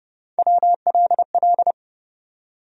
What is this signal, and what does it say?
Morse code, heard as a keyed tone, about 31 wpm, 700 Hz. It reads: WLL
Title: USA Call Sign Postfixes - 3 Letters 30wpm